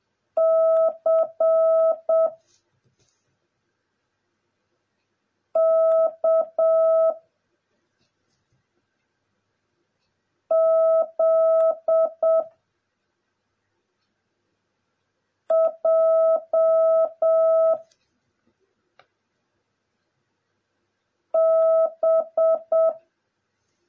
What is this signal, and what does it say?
Morse code, heard as a keyed tone, about 7 wpm, 650 Hz. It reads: C K Z J B
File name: jh8cbh / 2021cw / cw09a